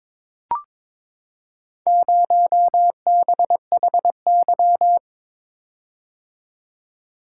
Morse code, heard as a keyed tone, about 22 wpm, 700 Hz. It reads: 0BHY
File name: practice / 22wpm / Four Letter-Number Combinations - 22wpm